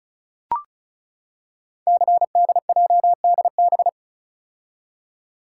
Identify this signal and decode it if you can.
Morse code, heard as a keyed tone, about 35 wpm, 700 Hz. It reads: CDJDB
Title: Code Groups - 5 Letters - 35wpm